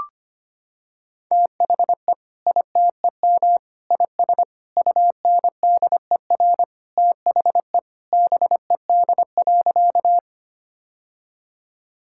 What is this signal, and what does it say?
Morse code, heard as a keyed tone, about 25 wpm, 700 Hz. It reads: THE ITEM IS UNDER THE BED.
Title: Sentences from Top 1000 Words - Review Part 6 - 25wpm